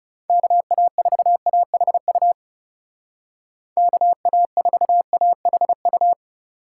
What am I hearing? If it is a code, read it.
Morse code, heard as a keyed tone, about 40 wpm, 700 Hz. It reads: KA4AHU KA4AHU